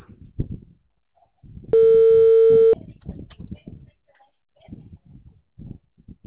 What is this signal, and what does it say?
Ringback tone (China, 450 Hz).